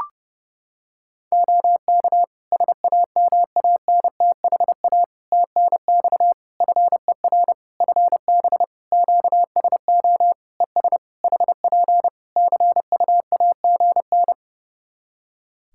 Morse code, heard as a keyed tone, about 30 wpm, 700 Hz. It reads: OK SAMANTHA TNX FER FB QSO ES HP CUAGN
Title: Practice QSO - Exchange 4 - Ending - 30wpm